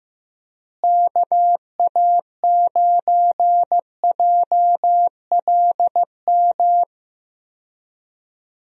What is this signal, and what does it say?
Morse code, heard as a keyed tone, about 15 wpm, 700 Hz. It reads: KA9JLM
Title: USA Call Signs - 6 Characters - Repeated 2x - 15wpm